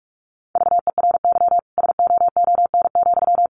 Morse code, heard as a keyed tone, about 44 wpm, 700 Hz. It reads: VERY SOON,
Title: Sentences from Top 1400 Words 45wpm